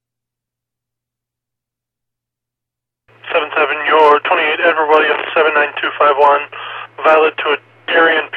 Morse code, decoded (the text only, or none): none